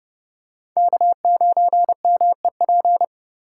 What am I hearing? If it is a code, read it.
Morse code, heard as a keyed tone, about 28 wpm, 700 Hz. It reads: K9MEP